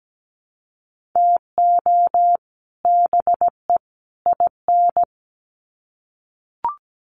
Morse code, heard as a keyed tone, about 17 wpm, 700 Hz. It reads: TO BE IN
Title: Sets of 3 Words 17wpm